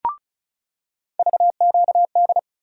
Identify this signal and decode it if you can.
Morse code, heard as a keyed tone, about 36 wpm, 700 Hz. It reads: VQD
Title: USA Call Sign Postfixes - 3 Letters 35wpm